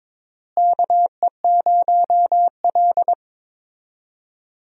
Morse code, heard as a keyed tone, about 22 wpm, 700 Hz. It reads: KE0L
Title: USA Call Signs - 4 Characters - Repeated 3x - 22wpm